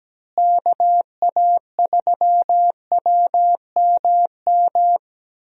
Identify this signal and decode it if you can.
Morse code, heard as a keyed tone, about 17 wpm, 700 Hz. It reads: KA3WMM
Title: USA Call Signs - 6 Characters - Repeated 2x - 17wpm